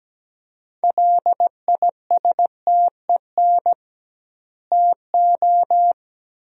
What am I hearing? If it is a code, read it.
Morse code, heard as a keyed tone, about 17 wpm, 700 Hz. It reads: LISTEN TO
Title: 2x Word-Spacing - Sentences from Top 400 Words - 17wpm